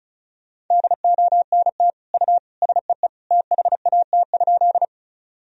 Morse code, heard as a keyed tone, about 36 wpm, 700 Hz. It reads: DONT U SEE THAT?
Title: Sentences from Top 300 Words and Abbreviations 35wpm